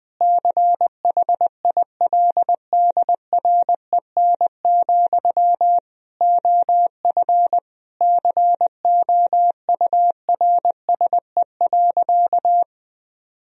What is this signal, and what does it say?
Morse code, heard as a keyed tone, about 20 wpm, 700 Hz. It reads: CHILDREN, OF COURSE.
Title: Sentences from Top 1000 Words - Review Part 1 - 20wpm